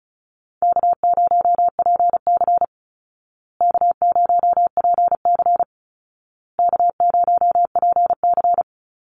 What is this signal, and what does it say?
Morse code, heard as a keyed tone, about 34 wpm, 700 Hz. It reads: K0PC K0PC K0PC
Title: Easy Call Signs Repeated 3x - 35wpm